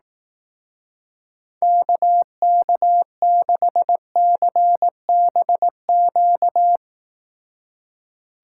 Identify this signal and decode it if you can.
Morse code, heard as a keyed tone, about 18 wpm, 700 Hz. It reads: KK6CBQ
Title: Speed Racing - USA 2x3 Call Signs 15wpm